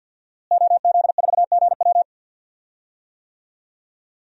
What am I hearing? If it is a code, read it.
Morse code, heard as a keyed tone, about 47 wpm, 700 Hz. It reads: OZ4GW
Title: DXCC-100 Call Signs - 5 Characters - Repeated 2x - 50wpm